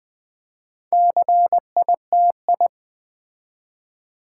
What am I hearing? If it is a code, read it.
Morse code, heard as a keyed tone, about 20 wpm, 700 Hz. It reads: CITI
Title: Build A Word - 4 Letters At A Time - 11-Letter Words - 20wpm